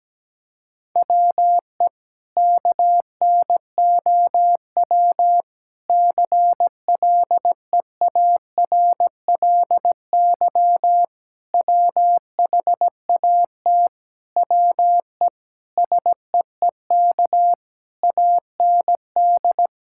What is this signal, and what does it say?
Morse code, heard as a keyed tone, about 17 wpm, 700 Hz. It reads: WE KNOW CLEARLY WHAT WE SEEK AND
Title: Sentences from Top 1600 Words 17wpm